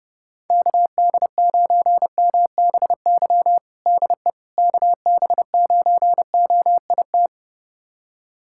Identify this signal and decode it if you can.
Morse code, heard as a keyed tone, about 32 wpm, 700 Hz. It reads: KD9MBY DE KB9OIT
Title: USA Callsign DE Callsign 30wpm